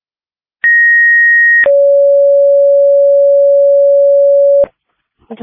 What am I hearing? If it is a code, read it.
Two-tone page, heard as a one-second tone, then a three-second tone, about 1900 then 560 Hz. It is heard from 0.6 to 4.6 s.